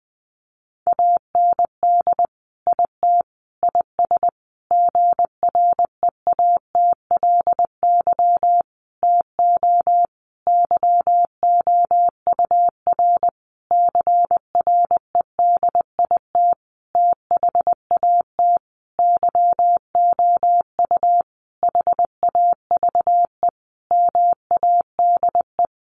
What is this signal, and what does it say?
Morse code, heard as a keyed tone, about 20 wpm, 700 Hz. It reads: AND IT IS GREATLY TO YOUR CREDIT THAT YOU HAVE MADE
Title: Sentences from Top 1400 Words 20wpm